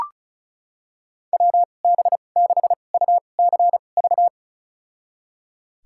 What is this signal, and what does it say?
Morse code, heard as a keyed tone, about 36 wpm, spaced as for 25 wpm, 700 Hz. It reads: WB6UCV
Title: USA Call Signs - 2x Character Spacing - 35wpm